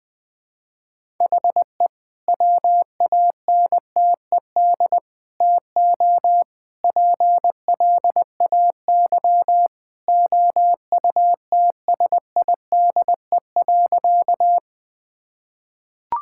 Morse code, heard as a keyed tone, about 21 wpm, 700 Hz. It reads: HE WANTED TO PLAY OUTSIDE.
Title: Sentences with 3-Word Phrases - Spoken Beginning - 20wpm